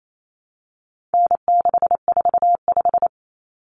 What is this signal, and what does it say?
Morse code, heard as a keyed tone, about 28 wpm, 700 Hz. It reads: N645
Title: Common Cut Numbers - 4-Digit 28wpm